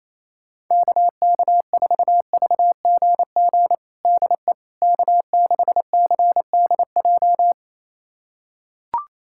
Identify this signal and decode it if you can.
Morse code, heard as a keyed tone, about 28 wpm, 700 Hz. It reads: KK4VGG DE K6CDJ
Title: USA Callsign DE Callsign 28wpm